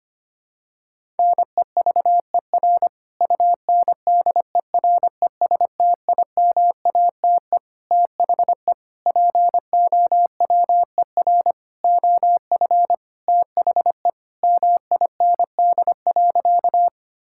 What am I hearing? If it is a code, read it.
Morse code, heard as a keyed tone, about 25 wpm, 700 Hz. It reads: NEVER UNDERESTIMATE THE POWER OF THE MIND.
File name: practice / 25wpm / Sentences with 4-Word Phrases - 25wpm